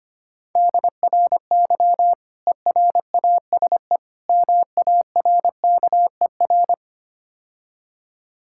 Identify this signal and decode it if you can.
Morse code, heard as a keyed tone, about 25 wpm, 700 Hz. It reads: DRY ERASE MARKER
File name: Easy Words - Speed-Racing with Hints - 25wpm